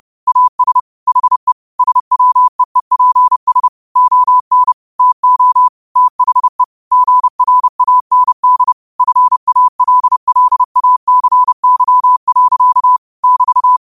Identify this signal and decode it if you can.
Morse code, heard as a keyed tone, about 30 wpm, 1000 Hz. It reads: AS HE SWEEPS ON TO THE GRAND FALLACY. <BT>
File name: qotdfiles / 30wpm